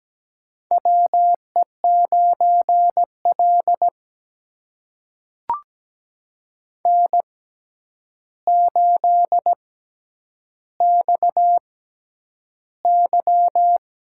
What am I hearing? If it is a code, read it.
Morse code, heard as a keyed tone, about 17 wpm, spaced as for 9 wpm, 700 Hz. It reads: WE9L N8XY
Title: USA Call Signs - 4 Characters - 6x Character Spacing - 17wpm